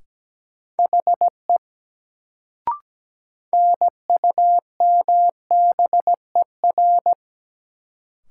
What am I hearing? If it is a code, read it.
Morse code, heard as a keyed tone, about 17 wpm, 700 Hz. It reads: HE NUMBER
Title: Sentences from Top 100 Words - New Word Drill - 17wpm